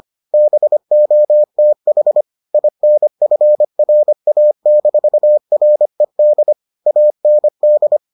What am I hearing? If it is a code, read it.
Morse code, heard as a keyed tone, about 25 wpm, 600 Hz. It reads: BOTH INFRA-RED AND